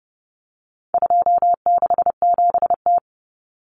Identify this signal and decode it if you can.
Morse code, heard as a keyed tone, about 30 wpm, 700 Hz. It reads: 267T
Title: Common Cut Numbers - 4-Digit 30wpm